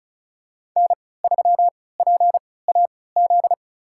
Morse code, heard as a keyed tone, about 36 wpm, spaced as for 20 wpm, 700 Hz. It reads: N3PAZ
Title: USA Call Signs - 5 Characters - 3x Character Spacing - 35wpm